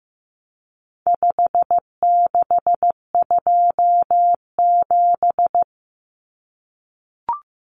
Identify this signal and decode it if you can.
Morse code, heard as a keyed tone, about 15 wpm, 700 Hz. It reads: 5627
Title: Common Cut Numbers - 4-Digit 15wpm